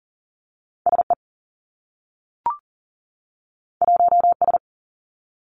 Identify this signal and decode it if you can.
Morse code, heard as a keyed tone, about 39 wpm, 700 Hz. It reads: S E 1 S
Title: Two Challenging Letter-Number Combinations 40wpm